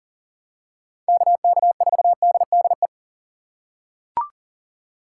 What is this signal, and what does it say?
Morse code, heard as a keyed tone, about 40 wpm, 700 Hz. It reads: KK4DDE
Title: USA Call Signs - 6 Characters - 2x Character Spacing - 40wpm